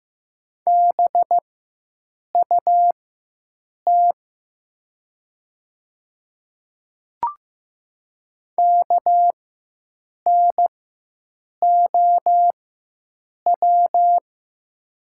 Morse code, heard as a keyed tone, about 15 wpm, spaced as for 7 wpm, 700 Hz. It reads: BUT KNOW
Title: 4x Character Spacing - Top 500 Words - 15wpm